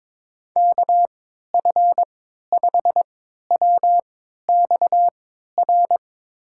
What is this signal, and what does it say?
Morse code, heard as a keyed tone, about 22 wpm, spaced as for 12 wpm, 700 Hz. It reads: KF5WXR
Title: USA Call Signs - 6 Characters - 3x Character Spacing - 22wpm